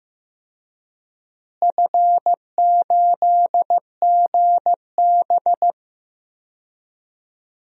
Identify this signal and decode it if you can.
Morse code, heard as a keyed tone, about 16 wpm, 700 Hz. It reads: F8GB
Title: DXCC-100 Call Signs - 4 Characters - Repeated 3x - 15wpm